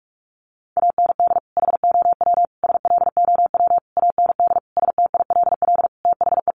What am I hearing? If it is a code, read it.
Morse code, heard as a keyed tone, about 44 wpm, 700 Hz. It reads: AND HOW SLOW AND STILL THE
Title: Sentences from Top 1500 Words 45wpm